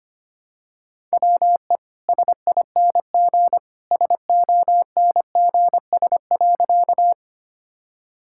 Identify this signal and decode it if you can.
Morse code, heard as a keyed tone, about 25 wpm, 690 Hz. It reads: WE SING SONGS.
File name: Easy Sentences with Hints - 25wpm